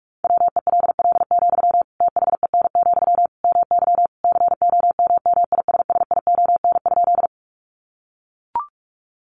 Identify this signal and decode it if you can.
Morse code, heard as a keyed tone, about 44 wpm, 700 Hz. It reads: WELL, THEN, MY COMMISSION?
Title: Sentences from Top 1400 Words 45wpm